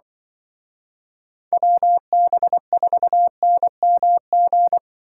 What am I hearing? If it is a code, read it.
Morse code, heard as a keyed tone, about 24 wpm, 700 Hz. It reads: WB4NMG
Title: Speed Racing - USA 2x3 Call Signs 20wpm